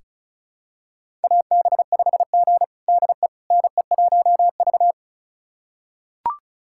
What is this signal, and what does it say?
Morse code, heard as a keyed tone, about 36 wpm, 690 Hz. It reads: AB5G DE NE1V
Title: USA Callsign DE Callsign - 4 Characters - 35wpm